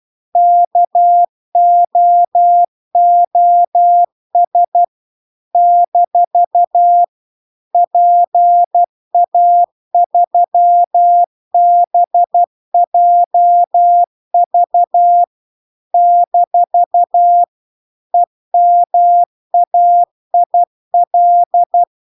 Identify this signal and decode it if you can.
Morse code, heard as a keyed tone, about 12 wpm, 700 Hz. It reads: KOOS - PA3BJV - EMAIL
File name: cw-cursus-VERON-a03